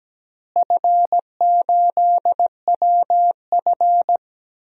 Morse code, heard as a keyed tone, about 17 wpm, 700 Hz. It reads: F8WF
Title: DXCC-100 Call Signs - 4 Characters - Repeated 2x - 17wpm